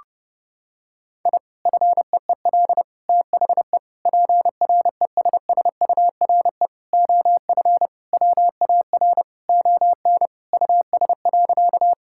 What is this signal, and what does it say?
Morse code, heard as a keyed tone, about 32 wpm, 700 Hz. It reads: I FEEL THE PRESSURE OF WAR ON US.
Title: Sentences from Top 1000 Words - Review Part 6 - 30wpm